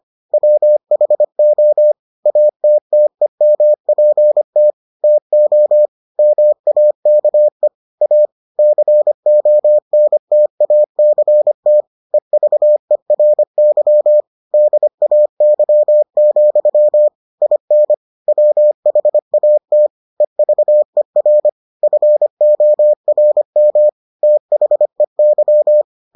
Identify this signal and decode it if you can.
Morse code, heard as a keyed tone, about 25 wpm, 600 Hz. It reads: WHO ATTEMPT TO MAKE A CONTACT EVERY DAY, IN WHAT EVER FORM THEY